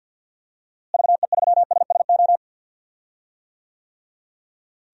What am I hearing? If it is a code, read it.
Morse code, heard as a keyed tone, about 51 wpm, 700 Hz. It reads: VE3SSO
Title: DXCC-100 Call Signs - 6 Characters - Repeated 3x - 50wpm